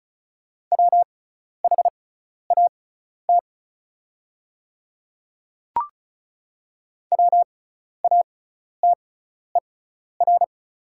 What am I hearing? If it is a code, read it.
Morse code, heard as a keyed tone, about 36 wpm, spaced as for 12 wpm, 690 Hz. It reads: WHAT WATER
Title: 6x Character Spacing - Top 200 Words - 35wpm